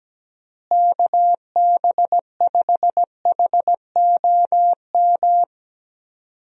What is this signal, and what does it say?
Morse code, heard as a keyed tone, about 17 wpm, 700 Hz. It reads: KB5HOM
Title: USA Call Signs - 6 Characters - Repeated 3x - 17wpm